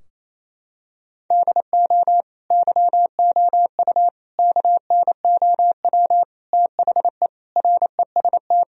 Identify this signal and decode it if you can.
Morse code, heard as a keyed tone, about 28 wpm, 700 Hz. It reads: DO YOU KNOW THE REST
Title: Sentences from Top 1000 Words - Review Part 2 - 28wpm